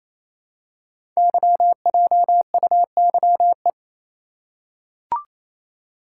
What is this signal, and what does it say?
Morse code, heard as a keyed tone, about 27 wpm, 700 Hz. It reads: YJUYE
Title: Code Groups - 5 Letters - 28wpm